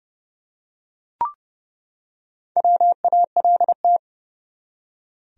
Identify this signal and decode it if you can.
Morse code, heard as a keyed tone, about 30 wpm, 700 Hz. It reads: WALT
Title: USA Common Names - Max 5 Letters - 30wpm